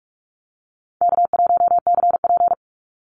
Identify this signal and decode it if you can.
Morse code, heard as a keyed tone, about 44 wpm, 700 Hz. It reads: K1CP
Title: Easy Call Signs Repeated 2x - 45wpm